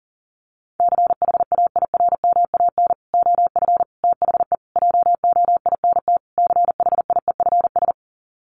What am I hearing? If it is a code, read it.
Morse code, heard as a keyed tone, about 40 wpm, 700 Hz. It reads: CHAIRMAN OF THE JOINT CHIEFS
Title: Sets of 5 Words Encore 40wpm